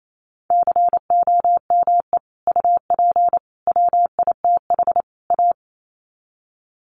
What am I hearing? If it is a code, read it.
Morse code, heard as a keyed tone, about 28 wpm, 700 Hz. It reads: COME UP WITH A
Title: Sets of 4 Words 28wpm